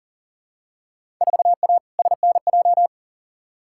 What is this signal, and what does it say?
Morse code, heard as a keyed tone, about 40 wpm, 700 Hz. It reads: 4A SNJ
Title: Field Day Exchanges Encore - 40wpm